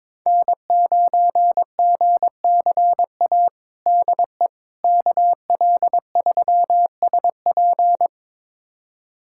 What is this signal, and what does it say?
Morse code, heard as a keyed tone, about 22 wpm, 700 Hz. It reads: N9GCA DE KL3SP
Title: USA Callsign DE Callsign - 5 Characters - 22wpm